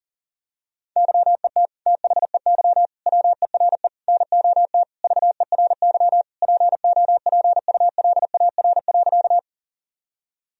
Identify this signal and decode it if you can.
Morse code, heard as a keyed tone, about 40 wpm, 700 Hz. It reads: YET THEY WERE NOT VERY POPULAR.
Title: Sentences from Top 1000 Words - Review Part 3 - 40wpm